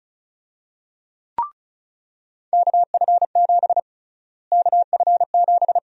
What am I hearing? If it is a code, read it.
Morse code, heard as a keyed tone, about 36 wpm, 700 Hz. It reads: KF7 KF7
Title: USA Call Sign Prefixes - Repeated 2x - 35wpm